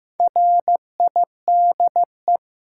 Morse code, heard as a keyed tone, about 15 wpm, 700 Hz. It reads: RIDE
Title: Sentences from Top 1200 Words - New Word Drill Alt - 15wpm